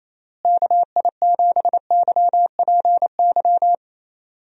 Morse code, heard as a keyed tone, about 28 wpm, 700 Hz. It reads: KI7YPY
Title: USA Call Signs - 6 Characters - Repeated 2x - 28wpm